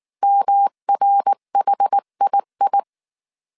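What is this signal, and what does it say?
Morse code, heard as a keyed tone, about 19 wpm, 800 Hz. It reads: MRHII
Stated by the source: Morse code dictation at 18 wpm from the IW3SGT QRS Morse code training course, five-character groups of 300 characters.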